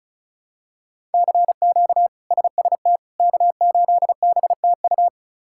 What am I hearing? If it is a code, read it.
Morse code, heard as a keyed tone, about 36 wpm, 700 Hz. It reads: CQ SST K8BTU
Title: K1USN SST QSO Practice - 35wpm